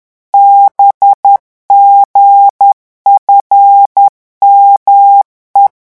Morse code, heard as a keyed tone, about 10 wpm, 800 Hz. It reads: BGFME
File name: morse mp3 / revision13-vitesse 10 mots